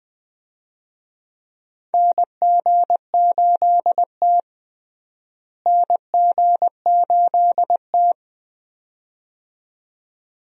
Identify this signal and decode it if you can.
Morse code, heard as a keyed tone, about 20 wpm, 700 Hz. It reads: NG8T NG8T
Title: USA Call Signs - 4 Characters - Repeated 2x - 20wpm